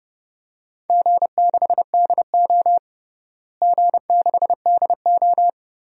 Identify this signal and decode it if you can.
Morse code, heard as a keyed tone, about 30 wpm, 700 Hz. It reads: G6DO G6DO
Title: DXCC-100 Call Signs - 4 Characters - Repeated 3x - 30wpm